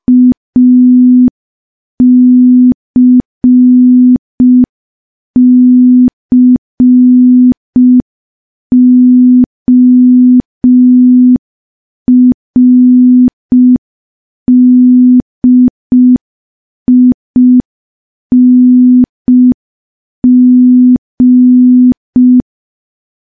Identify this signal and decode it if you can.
Morse code, heard as a keyed tone, about 5 wpm, 250 Hz. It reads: ACCORDING
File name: morse / morse-training-05WPM